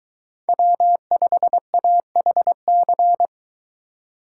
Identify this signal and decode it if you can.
Morse code, heard as a keyed tone, about 23 wpm, 700 Hz. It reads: W5AHC